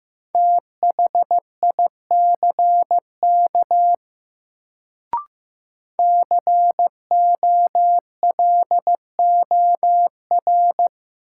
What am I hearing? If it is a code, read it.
Morse code, heard as a keyed tone, about 15 wpm, 700 Hz. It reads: THICK COLOR
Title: Sentences from Top 1600 Words - New Word Drill Alt - 15wpm